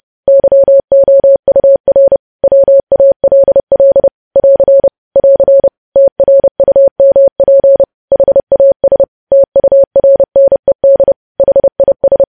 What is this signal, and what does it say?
Morse code, heard as a keyed tone, about 30 wpm, 550 Hz. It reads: YOUR WALL <AR> <AR> TRUMP HAS TURNED HIS
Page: news headlines in morse code at 30 wpm